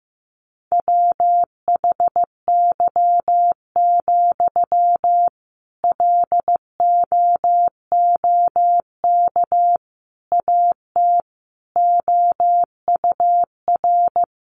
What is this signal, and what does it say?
Morse code, heard as a keyed tone, about 15 wpm, 700 Hz. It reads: WHY, LOOK AT OUR
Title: Sentences from Top 600 Words 15wpm